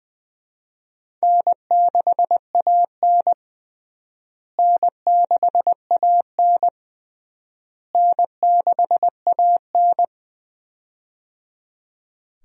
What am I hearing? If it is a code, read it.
Morse code, heard as a keyed tone, about 20 wpm, 700 Hz. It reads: N6AN N6AN N6AN
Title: USA Call Signs - 4 Characters - Repeated 3x - 20wpm